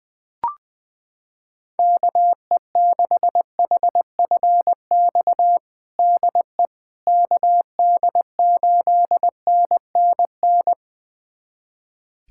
Morse code, heard as a keyed tone, about 20 wpm, 700 Hz. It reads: KE6HFX DE KD8NNN
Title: USA Callsign DE Callsign - 6 Characters - 20wpm